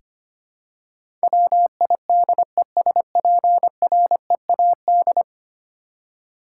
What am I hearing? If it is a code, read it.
Morse code, heard as a keyed tone, about 25 wpm, 700 Hz. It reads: WIDESPREAD
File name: 10 Letter Words - First 3 Letters Given - 25wpm